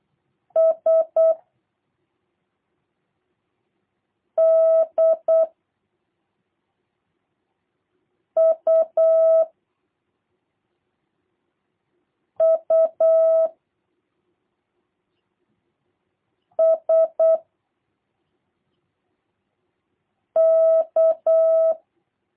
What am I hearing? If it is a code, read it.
Morse code, heard as a keyed tone, about 8 wpm, 650 Hz. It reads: S D U U S K
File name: jh8cbh / 2023cw / review1-3